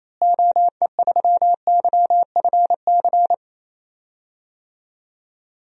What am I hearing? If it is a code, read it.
Morse code, heard as a keyed tone, about 27 wpm, 700 Hz. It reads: OE3YFC
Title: DXCC-100 Call Signs - 6 Characters - Repeated 2x - 28wpm